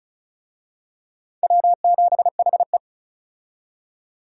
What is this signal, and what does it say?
Morse code, heard as a keyed tone, about 34 wpm, 700 Hz. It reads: W7HE
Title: USA Call Signs - 4 Characters - 35wpm